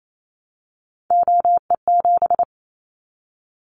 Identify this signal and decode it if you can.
Morse code, heard as a keyed tone, about 28 wpm, 700 Hz. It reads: OE7
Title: Callsign Prefixes for Top 100 DXCC - 28wpm